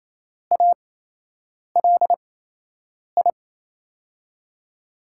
Morse code, heard as a keyed tone, about 28 wpm, spaced as for 8 wpm, 700 Hz. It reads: ALI
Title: USA Call Signs - 5 Characters - 8x Character Spacing - 28wpm